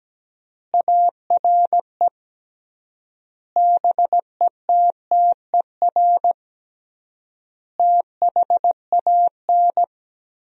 Morse code, heard as a keyed tone, about 17 wpm, 700 Hz. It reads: ARE BETTER THAN
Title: 3x Word-Spacing - Sentences from Top 400 Words - 17wpm